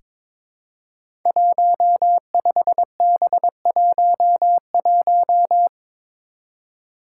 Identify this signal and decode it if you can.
Morse code, heard as a keyed tone, about 22 wpm, 700 Hz. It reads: 15B11
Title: Five Challenging Letter-Number Combinations 22wpm